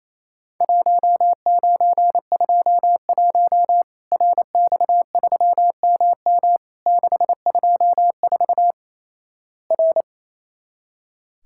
Morse code, heard as a keyed tone, about 27 wpm, 700 Hz. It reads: 1921 RX3MM 624 R
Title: Worked All Europe DX Contest - QTC Exchange - 28wpm